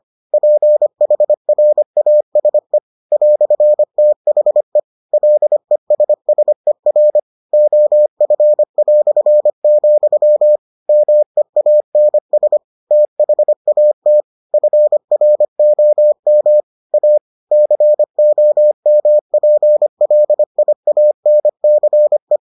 Morse code, heard as a keyed tone, about 25 wpm, 600 Hz. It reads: PHRASE "THE LESSER OF", MEANS THAT FROM A COMPLIANCE